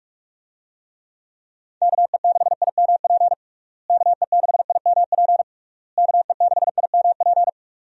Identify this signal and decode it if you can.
Morse code, heard as a keyed tone, about 46 wpm, 700 Hz. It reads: KE6IMP KE6IMP KE6IMP
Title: USA Call Signs - 6 Characters - Repeated 3x - 45wpm